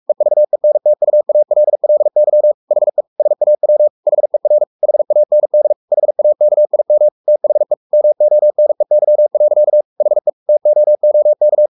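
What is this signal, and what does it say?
Morse code, heard as a keyed tone, about 44 wpm, 600 Hz. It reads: EVENTUALLY HE SAW HER HAND HAKIM THE MONEY. HE TOOK
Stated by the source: Morse code at 43 wpm sending Ken Follett's novel Never in English, from CW med Gnister.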